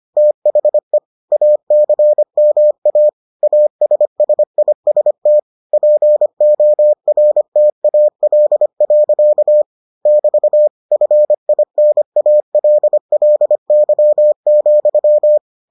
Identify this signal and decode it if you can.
Morse code, heard as a keyed tone, about 25 wpm, 600 Hz. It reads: THE ACMA ASSIST PORTAL. <BT> FINALLY,